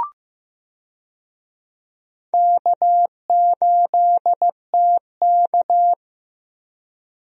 Morse code, heard as a keyed tone, about 15 wpm, 700 Hz. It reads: K8TK
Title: USA Call Signs - 4 Characters - Repeated 3x - 15wpm